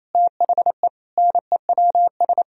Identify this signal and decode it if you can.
Morse code, heard as a keyed tone, about 28 wpm, 700 Hz. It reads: THE NEWS
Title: Sentences from Top 1000 Words - Review Part 4 - 28wpm